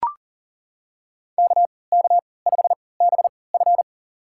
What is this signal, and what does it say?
Morse code, heard as a keyed tone, about 40 wpm, spaced as for 23 wpm, 700 Hz. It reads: KK5BF
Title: USA Call Signs - 5 Characters - 3x Character Spacing - 40wpm